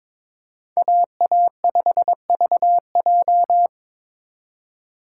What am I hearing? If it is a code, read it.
Morse code, heard as a keyed tone, about 25 wpm, 700 Hz. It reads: AA5VJ
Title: Speed Racing - USA 2x2 Call Signs 22wpm